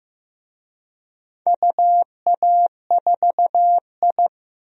Morse code, heard as a keyed tone, about 15 wpm, 700 Hz. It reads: UA4I
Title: DXCC-100 Call Signs - 4 Characters - Repeated 3x - 15wpm